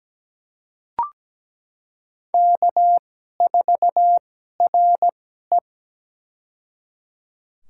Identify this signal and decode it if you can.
Morse code, heard as a keyed tone, about 17 wpm, spaced as for 12 wpm, 700 Hz. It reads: K4RE
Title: USA Call Signs - 4 Characters - 2x Character Spacing - 17wpm